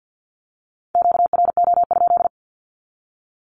Morse code, heard as a keyed tone, about 50 wpm, 700 Hz. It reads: QRO?
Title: Q-Codes 50wpm